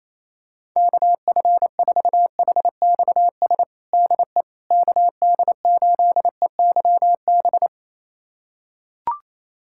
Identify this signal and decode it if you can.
Morse code, heard as a keyed tone, about 28 wpm, 700 Hz. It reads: KF4HXS DE KD8EYB
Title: USA Callsign DE Callsign - 6 Characters - 28wpm